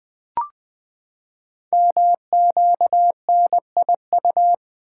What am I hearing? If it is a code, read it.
Morse code, heard as a keyed tone, about 20 wpm, 700 Hz. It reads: MQNIU
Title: Code Groups - 5 Letters - 20wpm